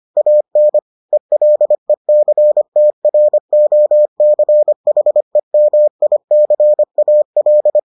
Morse code, heard as a keyed tone, about 25 wpm, 600 Hz. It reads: AN ELECTROCHEMICAL